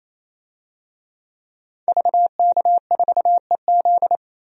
Morse code, heard as a keyed tone, about 28 wpm, 700 Hz. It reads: VK4EZ
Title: DXCC-100 Call Signs - 5 Characters - Repeated 3x - 28wpm